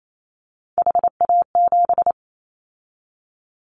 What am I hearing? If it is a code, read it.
Morse code, heard as a keyed tone, about 28 wpm, 700 Hz. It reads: HA7
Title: Callsign Prefixes for Top 25 DXCC - 28wpm